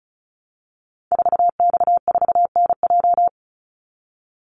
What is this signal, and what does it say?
Morse code, heard as a keyed tone, about 34 wpm, 700 Hz. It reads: 4X4NJ
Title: Call Signs - 35wpm